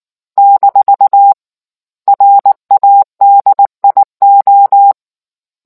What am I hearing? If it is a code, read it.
Morse code, heard as a keyed tone, about 19 wpm, spaced as for 15 wpm, 800 Hz. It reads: -RADIO